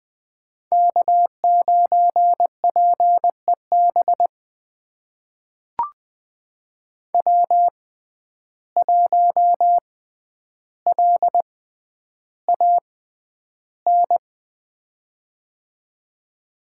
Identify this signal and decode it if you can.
Morse code, heard as a keyed tone, about 20 wpm, spaced as for 10 wpm, 700 Hz. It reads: K9PEB W1LAN
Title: USA Call Signs - 5 Characters - 6x Character Spacing - 20wpm